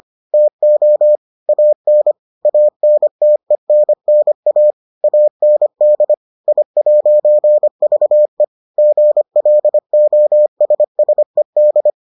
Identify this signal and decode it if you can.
Morse code, heard as a keyed tone, about 25 wpm, 600 Hz. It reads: TO AN ANTENNA AND I'VE GLOSSED